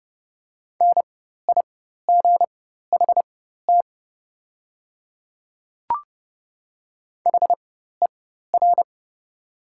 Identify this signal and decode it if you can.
Morse code, heard as a keyed tone, about 30 wpm, spaced as for 14 wpm, 700 Hz. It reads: NIGHT HER
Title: 4x Character Spacing - Top 300 Words - 30wpm